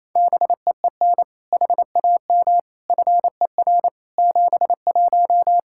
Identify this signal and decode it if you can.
Morse code, heard as a keyed tone, about 28 wpm, 700 Hz. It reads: BEEN HAM FER 71
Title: Practice QSO - Exchange 3 - More Info - 28wpm